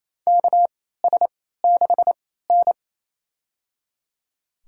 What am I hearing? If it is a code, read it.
Morse code, heard as a keyed tone, about 28 wpm, spaced as for 16 wpm, 700 Hz. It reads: KS6N
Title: USA Call Signs - 4 Characters - 3x Character Spacing - 28wpm